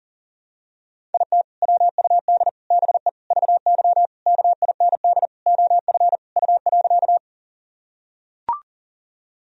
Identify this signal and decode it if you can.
Morse code, heard as a keyed tone, about 40 wpm, 700 Hz. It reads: IT WUD BE VY KIND OF U.
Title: Sentences from Top 200 Words and Abbreviations 40wpm